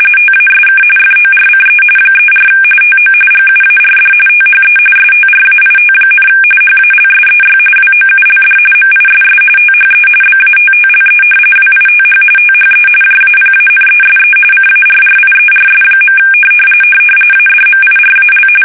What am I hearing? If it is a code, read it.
RTTY, about 45.45 baud. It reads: TASK FORCE 34 TO GUARD THE STRAIT, HALSEY WOULD MISS THE BATTLE WITH THE JAPANESE CARRIERS. ADMIRAL
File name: fsk850 / battleshipnj